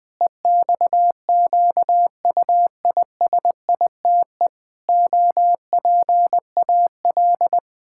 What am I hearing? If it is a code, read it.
Morse code, heard as a keyed tone, about 21 wpm, 700 Hz. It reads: EXQUISITE OPAL
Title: Pangram Challenge v2 - 20wpm